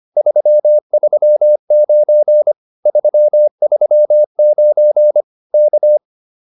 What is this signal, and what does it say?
Morse code, heard as a keyed tone, about 25 wpm, 590 Hz. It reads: 339 339 K
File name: POTA209a 25